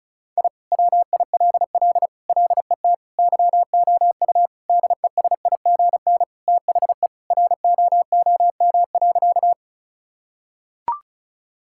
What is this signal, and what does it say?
Morse code, heard as a keyed tone, about 36 wpm, 690 Hz. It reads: I WILL LET YOU DESIGN THE ROOM.
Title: Sentences from Top 1000 Words - Review Part 5 - 35wpm